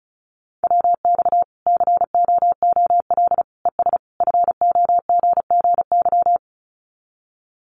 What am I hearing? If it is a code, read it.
Morse code, heard as a keyed tone, about 34 wpm, 700 Hz. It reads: WX COOL ES FOGGY
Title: Common QSO Phrases 35wpm